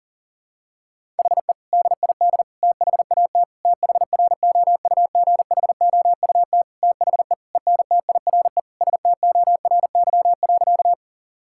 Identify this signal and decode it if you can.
Morse code, heard as a keyed tone, about 40 wpm, 700 Hz. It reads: HE DID THAT THROUGHOUT THE ENTIRE STORY.
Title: Sentences from Top 1000 Words - Review Part 5 - 40wpm